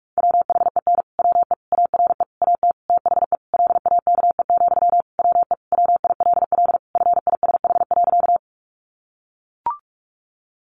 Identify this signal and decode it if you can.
Morse code, heard as a keyed tone, about 44 wpm, 700 Hz. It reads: WHEN WE ARE AT THE LAKE, WE WILL FISH.
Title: Sentences from Top 1100 Words 45wpm